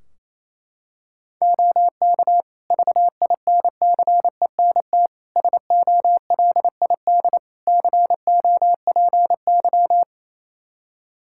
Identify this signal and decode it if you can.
Morse code, heard as a keyed tone, about 27 wpm, 700 Hz. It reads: OK VINCENT SOLID COPY
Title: Practice QSO - Exchange 3 - More Info - 28wpm